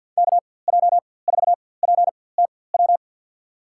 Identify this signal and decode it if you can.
Morse code, heard as a keyed tone, about 53 wpm, spaced as for 24 wpm, 700 Hz. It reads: KJ4PTW
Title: USA Call Signs - 6 Characters - 4x Character Spacing - 50wpm